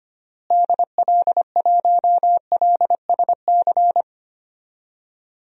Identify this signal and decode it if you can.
Morse code, heard as a keyed tone, about 25 wpm, 700 Hz. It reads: DL1LSC
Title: DXCC-100 Call Signs - 6 Characters - Repeated 2x - 25wpm